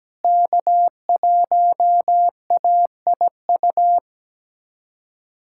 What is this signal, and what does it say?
Morse code, heard as a keyed tone, about 17 wpm, 700 Hz. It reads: K1AIU
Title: USA Callsign DE Callsign - 5 Characters - 17wpm